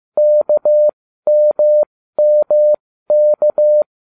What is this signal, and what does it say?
Morse code, heard as a keyed tone, about 15 wpm, spaced as for 13 wpm, 600 Hz. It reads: KMMK